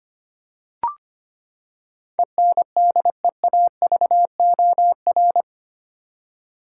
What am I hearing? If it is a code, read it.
Morse code, heard as a keyed tone, about 25 wpm, 700 Hz. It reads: ENDEAVOR
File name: Eight Letter Words - 25wpm